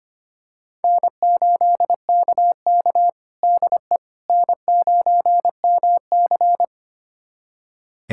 Morse code, heard as a keyed tone, about 25 wpm, 700 Hz. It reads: N8KK DE N9MC
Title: USA Callsign DE Callsign - 4 Characters - 25wpm